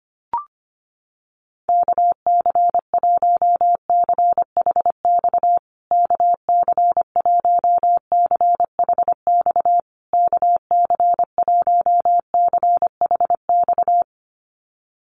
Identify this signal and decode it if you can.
Morse code, heard as a keyed tone, about 25 wpm, 700 Hz. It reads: KC1CHX KC1CHX KC1CHX
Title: USA Call Signs Repeated 3x 25wpm